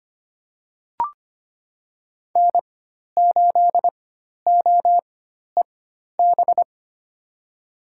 Morse code, heard as a keyed tone, about 25 wpm, spaced as for 12 wpm, 700 Hz. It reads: N8OEB
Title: USA Call Signs - 4x Character Spacing - 25wpm